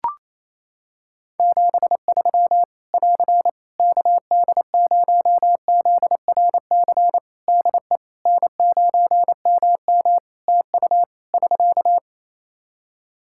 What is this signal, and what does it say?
Morse code, heard as a keyed tone, about 27 wpm, 700 Hz. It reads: 73 <AR> KD0ZRC DE N9MM TU <SK>
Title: Practice QSO - Exchange 4 - Ending - 28wpm